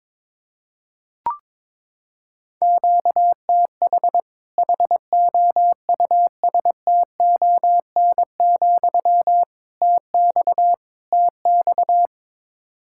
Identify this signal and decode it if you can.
Morse code, heard as a keyed tone, about 22 wpm, 700 Hz. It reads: QTH HOUSTON, TX TX
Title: Practice QSO - Exchange 1 - Beginning - 22wpm